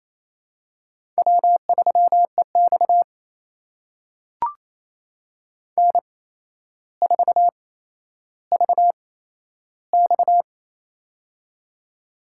Morse code, heard as a keyed tone, about 28 wpm, spaced as for 12 wpm, 700 Hz. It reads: W3EX N4VX
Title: USA Call Signs - 4 Characters - 8x Character Spacing - 28wpm